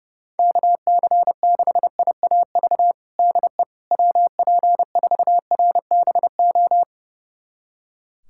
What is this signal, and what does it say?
Morse code, heard as a keyed tone, about 30 wpm, 700 Hz. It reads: KC6IAV DE WP4RBO
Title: USA Callsign DE Callsign 30wpm